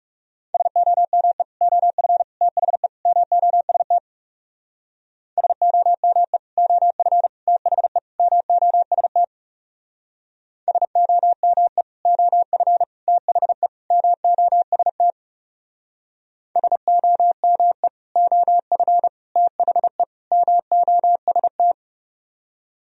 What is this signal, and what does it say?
Morse code, heard as a keyed tone, about 35 wpm, 700 Hz. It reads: SOME OF THE MOST SOME OF THE MOST SOME OF THE MOST SOME OF THE MOST
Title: Speed-Racing - Sets of 4 Words 30wpm